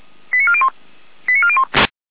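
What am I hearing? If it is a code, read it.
Five-tone selective call (ZVEI), heard as quick runs of five tones, tones of about 70 ms. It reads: 89401 89401
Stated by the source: ZVEI five-tone selective call 89722 received on an SDR.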